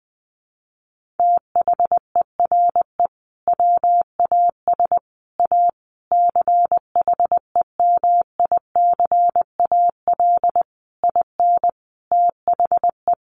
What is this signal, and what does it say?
Morse code, heard as a keyed tone, about 20 wpm, 700 Hz. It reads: THERE WAS A CHEMICAL IN THE
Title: Sentences from Top 1200 Words 20wpm